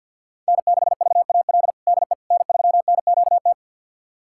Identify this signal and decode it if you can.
Morse code, heard as a keyed tone, about 51 wpm, 700 Hz. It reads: N6VAL DE N3NQT
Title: USA Callsign DE Callsign - 5 Characters - 50wpm